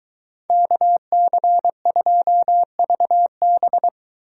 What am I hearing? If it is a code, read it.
Morse code, heard as a keyed tone, about 23 wpm, 700 Hz. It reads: KC2VB